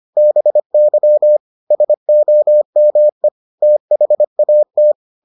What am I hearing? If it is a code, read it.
Morse code, heard as a keyed tone, about 25 wpm, 600 Hz. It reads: BY SOME THAT